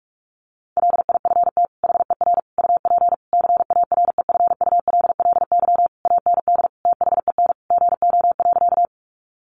Morse code, heard as a keyed tone, about 44 wpm, 700 Hz. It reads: LIFT HER UP CAREFULLY AND THEN GO.
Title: Sentences from Top 1300 Words 45wpm